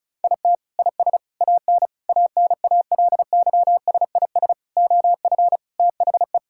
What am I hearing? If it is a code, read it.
Morse code, heard as a keyed tone, about 36 wpm, 700 Hz. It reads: IT IS AN ANALYSIS OF THE
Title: Sentences from Top 1000 Words - Review Part 5 - 35wpm